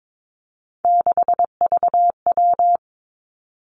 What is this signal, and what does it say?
Morse code, heard as a keyed tone, about 22 wpm, 700 Hz. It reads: 6VW
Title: Three Challenging Letter-Number Combinations 22wpm